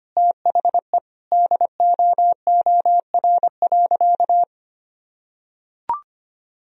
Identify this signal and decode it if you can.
Morse code, heard as a keyed tone, about 25 wpm, 700 Hz. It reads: THE DOOR.
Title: Sentences from Top 400 Words and Abbreviations 25wpm